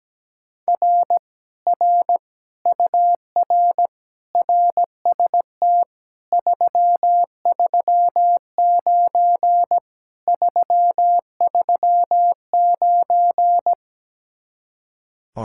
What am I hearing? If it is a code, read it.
Morse code, heard as a keyed tone, about 17 wpm, 700 Hz. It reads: R R UR RST 339 339
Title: Speed-Racing Common QSO Phrases Encore - 17wpm